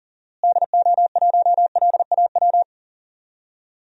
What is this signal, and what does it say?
Morse code, heard as a keyed tone, about 40 wpm, 700 Hz. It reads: DO1LAW
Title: DXCC-100 Call Signs - 6 Characters - Repeated 2x - 40wpm